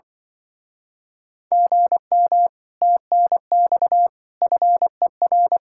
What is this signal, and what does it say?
Morse code, heard as a keyed tone, about 24 wpm, 700 Hz. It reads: GM TNX FER
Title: Speed-Racing Common QSO Phrases - 20wpm